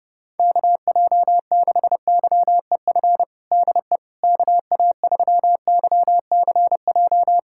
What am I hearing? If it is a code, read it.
Morse code, heard as a keyed tone, about 30 wpm, 690 Hz. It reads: KJ6YEF DE KA3YCJ
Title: USA Callsign DE Callsign 30wpm